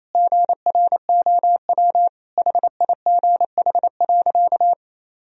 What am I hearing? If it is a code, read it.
Morse code, heard as a keyed tone, about 28 wpm, 700 Hz. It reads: GROW HIGH.
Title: Sentences from Top 1000 Words - Review Part 1 - 28wpm